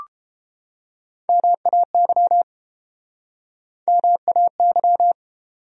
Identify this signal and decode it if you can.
Morse code, heard as a keyed tone, about 33 wpm, 700 Hz. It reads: MAY MAY